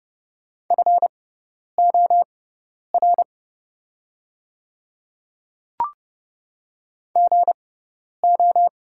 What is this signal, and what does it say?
Morse code, heard as a keyed tone, about 29 wpm, spaced as for 10 wpm, 700 Hz. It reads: FOR GO
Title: 6x Character Spacing - Top 100 Words - 30wpm